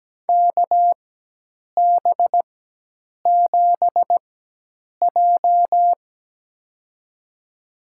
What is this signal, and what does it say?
Morse code, heard as a keyed tone, about 17 wpm, spaced as for 8 wpm, 700 Hz. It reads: KB7J
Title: USA Call Signs - 4x Character Spacing - 17wpm